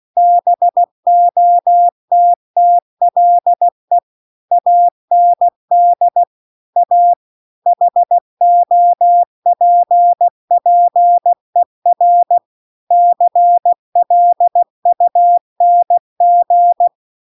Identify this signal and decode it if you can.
Morse code, heard as a keyed tone, about 16 wpm, 700 Hz. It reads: BOTTLE AND A HOPPER CLUNG